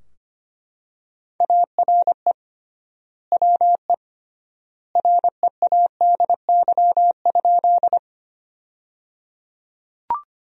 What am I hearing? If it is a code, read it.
Morse code, heard as a keyed tone, about 25 wpm, 700 Hz. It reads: ARE WE READY?
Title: 3x Word-Spacing - Sentences from Top 400 Words - 25wpm